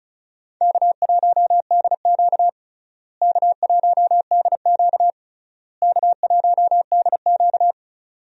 Morse code, heard as a keyed tone, about 34 wpm, 700 Hz. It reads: K1DQ K1DQ K1DQ
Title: USA Call Signs - 4 Characters - Repeated 3x - 35wpm